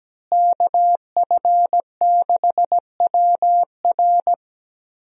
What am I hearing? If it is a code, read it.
Morse code, heard as a keyed tone, about 17 wpm, 700 Hz. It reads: KF6WR
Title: USA Call Signs - 5 Characters - Repeated 2x - 17wpm